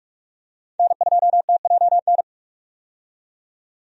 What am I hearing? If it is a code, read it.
Morse code, heard as a keyed tone, about 46 wpm, 700 Hz. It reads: N2TJN
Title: USA Call Signs - 5 Characters - 45wpm